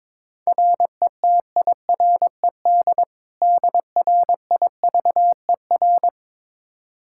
Morse code, heard as a keyed tone, about 22 wpm, 700 Hz. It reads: RETIRED DRIVER
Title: Practice QSO - Exchange 3 - More Info - 22wpm